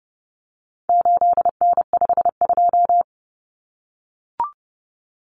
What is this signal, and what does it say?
Morse code, heard as a keyed tone, about 30 wpm, 700 Hz. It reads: 8N52
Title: Common Cut Numbers - 4-Digit 30wpm